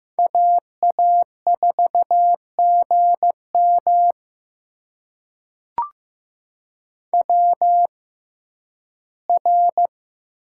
Morse code, heard as a keyed tone, about 15 wpm, spaced as for 11 wpm, 700 Hz. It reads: AA4GM WR
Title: USA Call Signs - 6x Character Spacing - 15wpm